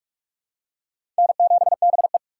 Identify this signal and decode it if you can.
Morse code, heard as a keyed tone, about 47 wpm, 700 Hz. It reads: N7BE
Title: USA Call Signs - 4 Characters - 4x Character Spacing - 45wpm